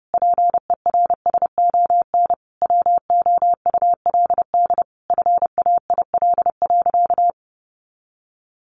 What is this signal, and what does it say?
Morse code, heard as a keyed tone, about 30 wpm, 700 Hz. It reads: PERSON WOULD FAIL.
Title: Sentences from Top 1100 Words 30wpm